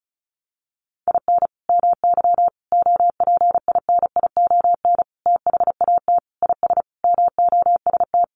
Morse code, heard as a keyed tone, about 34 wpm, 700 Hz. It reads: IN MY OPINION THAT IS MOST
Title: Sentences from Top 1200 Words 35wpm